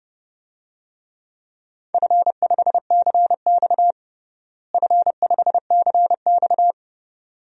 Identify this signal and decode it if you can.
Morse code, heard as a keyed tone, about 32 wpm, 700 Hz. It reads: F5CX F5CX
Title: DXCC-100 Call Signs - 4 Characters - Repeated 2x - 30wpm